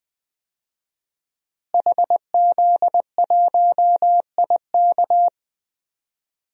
Morse code, heard as a keyed tone, about 20 wpm, 700 Hz. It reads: HZ1IK
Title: DXCC-100 Call Signs - 5 Characters - Repeated 3x - 20wpm